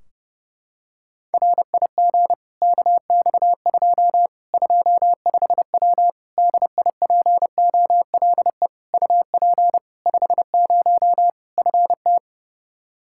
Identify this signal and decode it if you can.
Morse code, heard as a keyed tone, about 30 wpm, 700 Hz. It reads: RIG KX2 25W DIPOLE UP 50 FT
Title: K1USN SST QSO Practice 30wpm